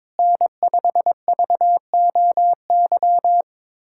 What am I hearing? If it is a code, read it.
Morse code, heard as a keyed tone, about 22 wpm, 700 Hz. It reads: N5VOY
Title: USA Callsign DE Callsign - 5 Characters - 22wpm